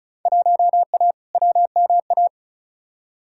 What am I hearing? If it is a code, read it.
Morse code, heard as a keyed tone, about 36 wpm, 700 Hz. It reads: 1A WMA
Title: Field Day Exchanges - 35wpm